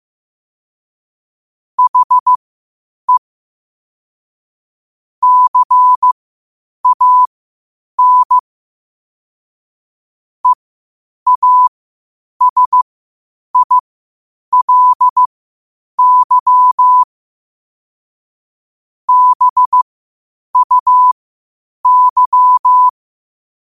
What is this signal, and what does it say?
Morse code, heard as a keyed tone, about 15 wpm, spaced as for 8 wpm, 1000 Hz. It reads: HE CAN EASILY BUY
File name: qotdfiles / 8wpm